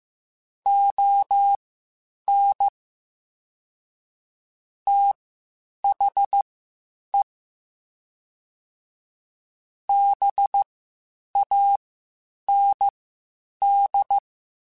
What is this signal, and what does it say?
Morse code, heard as a keyed tone, about 15 wpm, spaced as for 8 wpm, 800 Hz. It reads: ON THE BAND
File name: (spaced) / QSOs Spaced-mp3s / WD0BGP 1 @13 sp